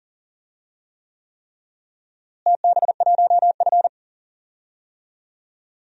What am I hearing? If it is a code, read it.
Morse code, heard as a keyed tone, about 40 wpm, 700 Hz. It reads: TB1F
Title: DXCC-100 Call Signs - 4 Characters - Repeated 3x - 40wpm